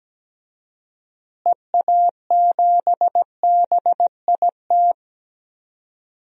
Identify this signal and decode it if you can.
Morse code, heard as a keyed tone, about 17 wpm, 700 Hz. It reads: EA7BIT
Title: DXCC-100 Call Signs - 6 Characters - Repeated 2x - 17wpm